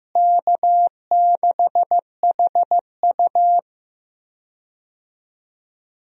Morse code, heard as a keyed tone, about 16 wpm, 700 Hz. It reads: K6HU
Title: USA Call Signs - 4 Characters - Repeated 3x - 15wpm